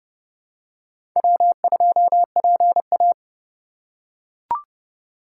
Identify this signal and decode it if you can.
Morse code, heard as a keyed tone, about 30 wpm, 700 Hz. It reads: W2PA
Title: USA Call Signs - 4 Characters - 2x Character Spacing - 30wpm